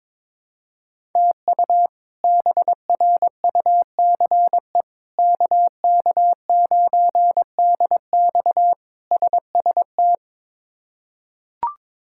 Morse code, heard as a keyed tone, about 22 wpm, 700 Hz. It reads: TU BRUCE KK9DX SST
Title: K1USN SST QSO Practice - 22wpm